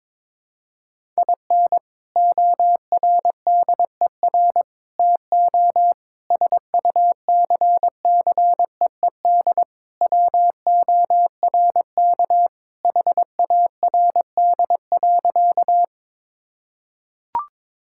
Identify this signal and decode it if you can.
Morse code, heard as a keyed tone, about 22 wpm, 700 Hz. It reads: IN ORDER TO SUCCEED WORK HARD.
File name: practice / 22wpm / Sentences with 3-Word Phrases - 22wpm